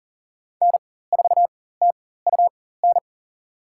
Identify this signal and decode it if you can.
Morse code, heard as a keyed tone, about 40 wpm, spaced as for 19 wpm, 700 Hz. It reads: N4TUN
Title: USA Call Signs - 5 Characters - 4x Character Spacing - 40wpm